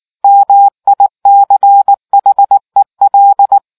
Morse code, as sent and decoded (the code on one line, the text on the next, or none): -- .. -.-. .... .-..
MICHEL